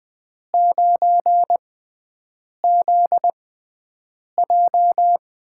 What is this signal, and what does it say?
Morse code, heard as a keyed tone, about 20 wpm, spaced as for 7 wpm, 700 Hz. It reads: 9ZJ
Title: USA Call Signs - 5 Characters - 6x Character Spacing - 20wpm